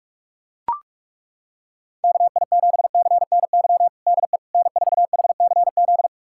Morse code, heard as a keyed tone, about 42 wpm, 700 Hz. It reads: KI7CNY DE N4HCZ
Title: USA Callsign DE Callsign 45wpm